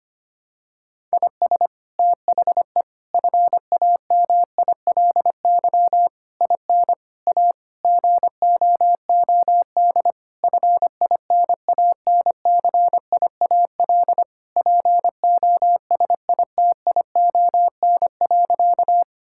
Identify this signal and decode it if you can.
Morse code, heard as a keyed tone, about 25 wpm, 700 Hz. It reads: IS THE FAMILY IN A GOOD FINANCIAL POSITION.
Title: Sentences from Top 1000 Words - Review Part 5 - 25wpm